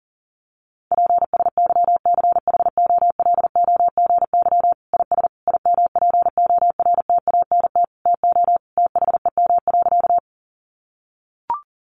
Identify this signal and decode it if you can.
Morse code, heard as a keyed tone, about 40 wpm, 700 Hz. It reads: PSYCHOLOGY IS IMPORTANT TO THEM.
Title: Sentences from Top 1400 Words 40wpm